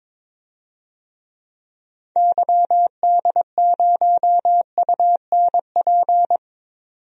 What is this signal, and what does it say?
Morse code, heard as a keyed tone, about 22 wpm, 700 Hz. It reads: YD0UNP
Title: DXCC-100 Call Signs - 6 Characters - 22wpm